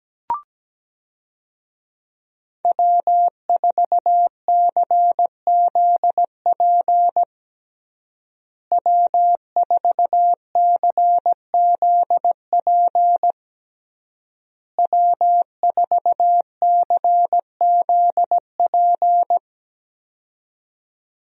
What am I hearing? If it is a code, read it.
Morse code, heard as a keyed tone, about 17 wpm, 700 Hz. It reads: W4CZP W4CZP W4CZP
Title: USA Call Signs - 5 Characters - Repeated 3x - 17wpm